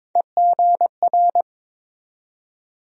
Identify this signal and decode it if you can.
Morse code, heard as a keyed tone, about 22 wpm, 700 Hz. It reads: EGR
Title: USA Call Sign Postfixes - 3 Letters 22wpm